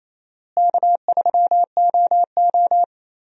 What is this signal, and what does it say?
Morse code, heard as a keyed tone, about 28 wpm, 700 Hz. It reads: K3OO
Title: USA Call Signs - 4 Characters - Repeated 3x - 28wpm